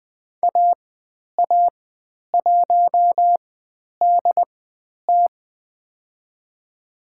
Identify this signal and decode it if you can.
Morse code, heard as a keyed tone, about 20 wpm, spaced as for 10 wpm, 700 Hz. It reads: AA1DT
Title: 102 Lesson 17g - Letter u - Call Signs 20wpm